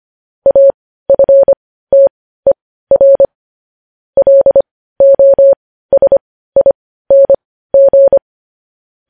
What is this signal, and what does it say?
Morse code, heard as a keyed tone, about 25 wpm, spaced as for 15 wpm, 550 Hz. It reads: AFTER LOSING
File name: podcast / News headlines-1x-25x15